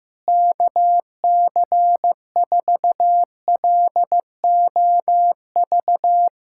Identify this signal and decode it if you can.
Morse code, heard as a keyed tone, about 15 wpm, 700 Hz. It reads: KC4LOV
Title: USA Callsign DE Callsign 15wpm